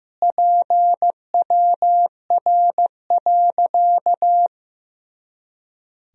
Morse code, heard as a keyed tone, about 15 wpm, 700 Hz. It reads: PWR.
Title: Sentences from Top 500 Words and Abbreviations 15wpm